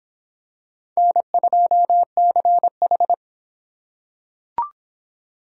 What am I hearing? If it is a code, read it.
Morse code, heard as a keyed tone, about 22 wpm, 700 Hz. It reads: N2CH